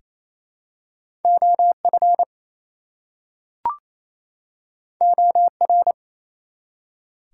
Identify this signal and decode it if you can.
Morse code, heard as a keyed tone, about 27 wpm, 700 Hz. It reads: OF OR
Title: Two Letter Words - 28wpm